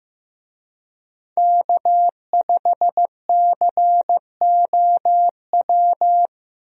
Morse code, heard as a keyed tone, about 15 wpm, 700 Hz. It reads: K5COW
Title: USA Call Signs - 5 Characters - 15wpm